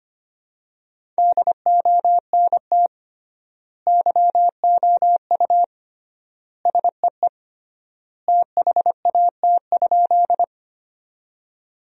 Morse code, heard as a keyed tone, about 25 wpm, 700 Hz. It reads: DONT YOU SEE THAT?
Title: 3x Word-Spacing - Sentences from Top 300 Words - 25wpm